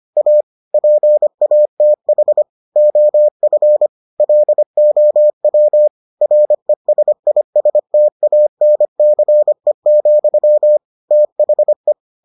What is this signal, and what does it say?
Morse code, heard as a keyed tone, about 25 wpm, 600 Hz. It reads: A PATH OF LOW RESISTANCE, THE